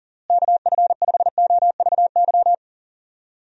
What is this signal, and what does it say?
Morse code, heard as a keyed tone, about 40 wpm, 700 Hz. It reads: KF5OVY
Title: USA Call Signs - 6 Characters - Repeated 2x - 40wpm